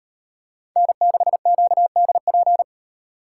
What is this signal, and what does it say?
Morse code, heard as a keyed tone, about 38 wpm, 700 Hz. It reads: N6QDP